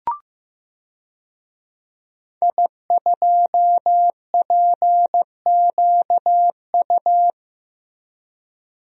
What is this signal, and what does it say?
Morse code, heard as a keyed tone, about 15 wpm, 700 Hz. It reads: I2PQU
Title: DXCC-100 Call Signs - 5 Characters - Repeated 3x - 15wpm